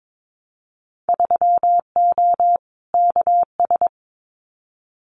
Morse code, heard as a keyed tone, about 22 wpm, 700 Hz. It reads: 3O KS
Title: Winter Field Day Exchanges Encore 22wpm